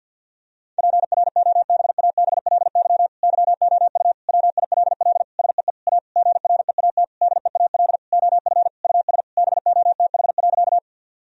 Morse code, heard as a keyed tone, about 50 wpm, 700 Hz. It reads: PROBABLY YOU WILL SEE A GREAT DEAL OF US BOTH.
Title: Sentences from Top 1000 Words - Review Part 4 - 50wpm